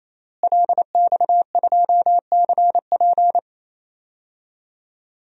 Morse code, heard as a keyed tone, about 27 wpm, 700 Hz. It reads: LX2CP
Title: DXCC-100 Call Signs - 5 Characters - Repeated 3x - 28wpm